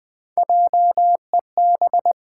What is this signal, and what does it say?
Morse code, heard as a keyed tone, about 20 wpm, 700 Hz. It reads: JEB
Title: Pangram Challenge v2 - 20wpm